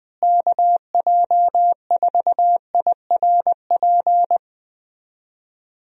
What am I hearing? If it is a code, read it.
Morse code, heard as a keyed tone, about 20 wpm, 700 Hz. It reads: KJ4IRP
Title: USA Call Signs - 6 Characters - Repeated 3x - 20wpm